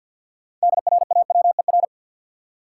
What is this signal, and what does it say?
Morse code, heard as a keyed tone, about 52 wpm, 700 Hz. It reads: DRAWER
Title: Six Letter Words - 50wpm